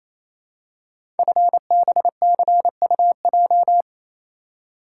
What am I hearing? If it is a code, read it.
Morse code, heard as a keyed tone, about 28 wpm, 700 Hz. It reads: FBCUJ
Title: Code Groups - 5 Letters - 28wpm